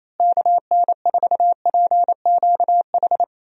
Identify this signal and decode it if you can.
Morse code, heard as a keyed tone, about 28 wpm, 700 Hz. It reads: KN4PQH
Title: USA Call Signs - 6 Characters - Repeated 2x - 28wpm